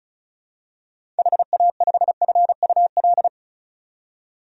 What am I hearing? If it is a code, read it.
Morse code, heard as a keyed tone, about 34 wpm, 700 Hz. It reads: HA5FUL
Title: DXCC-100 Call Signs - 6 Characters - 35wpm